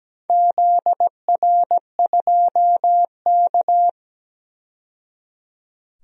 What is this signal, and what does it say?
Morse code, heard as a keyed tone, about 17 wpm, 700 Hz. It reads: ZR2K
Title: DXCC-100 Call Signs - 4 Characters - 17wpm